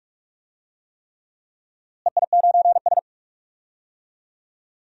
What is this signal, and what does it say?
Morse code, heard as a keyed tone, about 46 wpm, 700 Hz. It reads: EI9S
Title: DXCC-100 Call Signs - 4 Characters - Repeated 3x - 45wpm